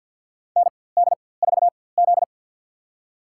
Morse code, heard as a keyed tone, about 53 wpm, spaced as for 24 wpm, 700 Hz. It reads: ND4Z
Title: USA Call Signs - 4x Character Spacing - 50wpm